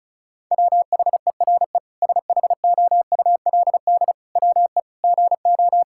Morse code, heard as a keyed tone, about 34 wpm, 700 Hz. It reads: WHERE SHOULD WE GO
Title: Sentences from Top 1000 Words - Review Part 5 - 35wpm